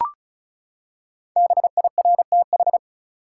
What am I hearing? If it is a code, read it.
Morse code, heard as a keyed tone, about 36 wpm, 700 Hz. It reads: BIRTH
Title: Five Letter Words - 35wpm